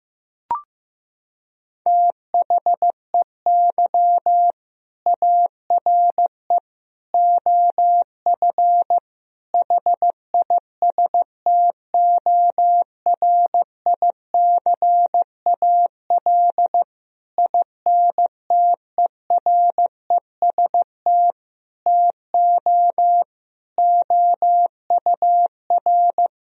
Morse code, heard as a keyed tone, about 15 wpm, 700 Hz. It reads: THEY ARE OF HISTORICAL INTEREST TO OUR
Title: Sentences from Top 1000 Words - Review Part 4 - 15wpm